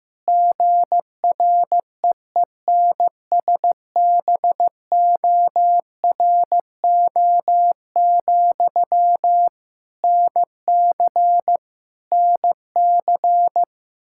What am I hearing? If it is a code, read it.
Morse code, heard as a keyed tone, about 15 wpm, 700 Hz. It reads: GREENSBORO, NC NC
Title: Practice QSO - Exchange 1 - Beginning - 15wpm